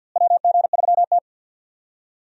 Morse code, heard as a keyed tone, about 51 wpm, 700 Hz. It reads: WG3T
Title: USA Call Signs - 4 Characters - Repeated 2x - 50wpm